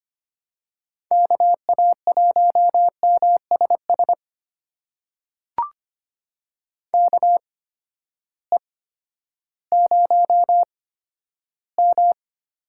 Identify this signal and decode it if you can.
Morse code, heard as a keyed tone, about 25 wpm, spaced as for 12 wpm, 700 Hz. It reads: KA1MSS KE0M
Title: USA Call Signs - 6 Characters - 8x Character Spacing - 25wpm